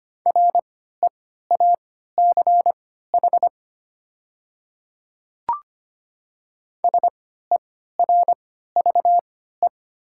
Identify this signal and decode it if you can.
Morse code, heard as a keyed tone, about 25 wpm, spaced as for 14 wpm, 690 Hz. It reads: REACH SERVE
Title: 3x Character Spacing - Top 500 Words - 25wpm